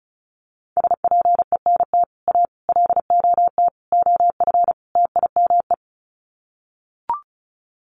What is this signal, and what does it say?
Morse code, heard as a keyed tone, about 34 wpm, 700 Hz. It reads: SPENT A LOT OF TIME
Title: Sets of 5 Words 35wpm